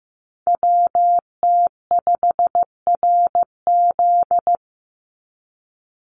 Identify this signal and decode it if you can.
Morse code, heard as a keyed tone, about 15 wpm, 700 Hz. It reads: WT5RZ
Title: Call Signs Repeated 3x - 15wpm